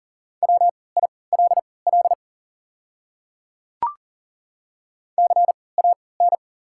Morse code, heard as a keyed tone, about 40 wpm, spaced as for 23 wpm, 690 Hz. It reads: WILL CAN
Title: 3x Character Spacing - Top 300 Words - 40wpm